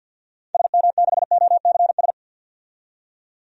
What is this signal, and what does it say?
Morse code, heard as a keyed tone, about 52 wpm, 700 Hz. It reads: SM6OCS
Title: DXCC-100 Call Signs - 6 Characters - 50wpm